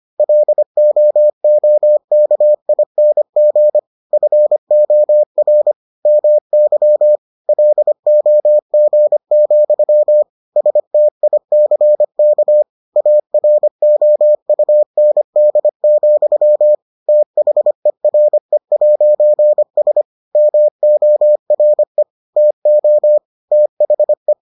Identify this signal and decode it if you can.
Morse code, heard as a keyed tone, about 25 wpm, 600 Hz. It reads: LOOKING FOR MY LOG, STICK AROUND, THERE'S MORE TO THE